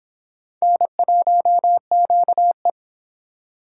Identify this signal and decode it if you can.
Morse code, heard as a keyed tone, about 25 wpm, 700 Hz. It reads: N1QE